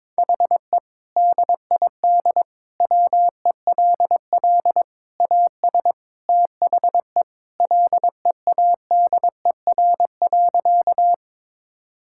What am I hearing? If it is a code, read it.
Morse code, heard as a keyed tone, about 21 wpm, 700 Hz. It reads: HE DID WELL AS THE LEADER.
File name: practice / 22wpm / Sentences with 3-Word Phrases - 22wpm